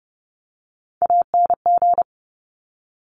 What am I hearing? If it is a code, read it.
Morse code, heard as a keyed tone, about 30 wpm, 700 Hz. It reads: ANG
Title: Top Three Letter Combinations in Words 30wpm